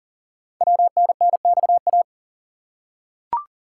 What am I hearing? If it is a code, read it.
Morse code, heard as a keyed tone, about 40 wpm, 690 Hz. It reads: WNNXA
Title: Code Groups - 5 Letters - 40wpm